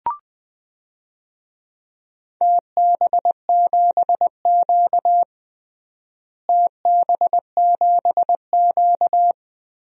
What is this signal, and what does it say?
Morse code, heard as a keyed tone, about 20 wpm, 700 Hz. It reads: TB7Q TB7Q
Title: DXCC-100 Call Signs - 4 Characters - Repeated 2x - 20wpm